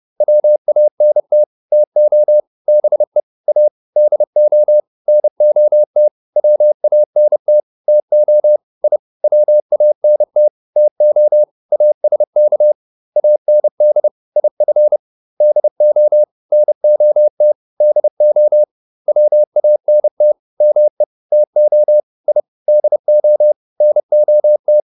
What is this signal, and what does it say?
Morse code, heard as a keyed tone, about 30 wpm, 600 Hz. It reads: WANT TO BE A DO NOT WANT TO I WANT TO ASK AND IF DO NOT DO WANT ME TO I DO NOT